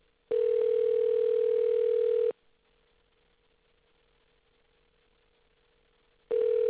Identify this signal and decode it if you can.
Ringback tone (North America), 440 + 480 Hz.
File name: things / ringback crosstalk maybe